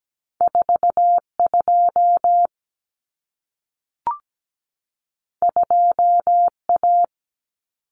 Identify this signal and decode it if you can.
Morse code, heard as a keyed tone, about 17 wpm, 700 Hz. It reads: 42 2A
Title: Common Cut Numbers - 2-Digit 17wpm